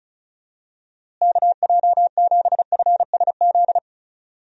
Morse code, heard as a keyed tone, about 34 wpm, 700 Hz. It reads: KJ7FSZ
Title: USA Call Signs - 6 Characters - Repeated 2x - 35wpm